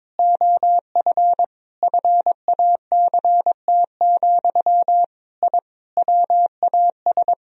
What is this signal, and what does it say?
Morse code, heard as a keyed tone, about 22 wpm, 700 Hz. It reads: OF FACT, I WAS
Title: Sentences from Top 1000 Words - Review Part 4 - 22wpm